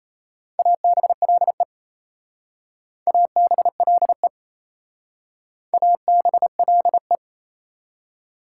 Morse code, heard as a keyed tone, about 37 wpm, 700 Hz. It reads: ABLE ABLE ABLE